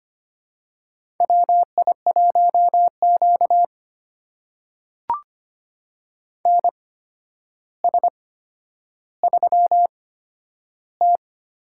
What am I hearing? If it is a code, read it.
Morse code, heard as a keyed tone, about 25 wpm, spaced as for 11 wpm, 700 Hz. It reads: WI1Q NS3T
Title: USA Call Signs - 4 Characters - 8x Character Spacing - 25wpm